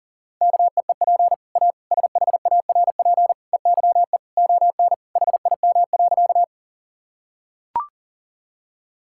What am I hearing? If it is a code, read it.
Morse code, heard as a keyed tone, about 40 wpm, 700 Hz. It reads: KEEP A SHARP EYE ON HIM.
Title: Sentences from Top 1600 Words 40wpm